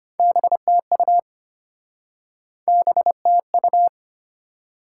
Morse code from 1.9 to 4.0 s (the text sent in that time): BTU